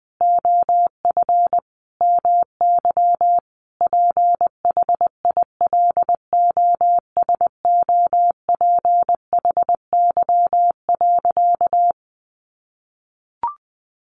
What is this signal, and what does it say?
Morse code, heard as a keyed tone, about 20 wpm, 700 Hz. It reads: OF MY PHILOSOPHY.
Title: Sentences from Top 900 Words 20wpm